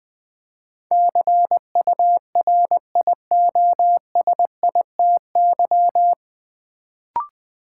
Morse code, heard as a keyed tone, about 20 wpm, 700 Hz. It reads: CURIOSITY
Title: Build A Word - 3 Letters At A Time - 9-Letter Words - 20wpm